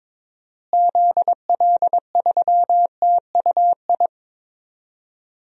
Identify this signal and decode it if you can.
Morse code, heard as a keyed tone, about 21 wpm, 700 Hz. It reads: ZL3TUI
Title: DXCC-100 Call Signs - 6 Characters - Repeated 2x - 22wpm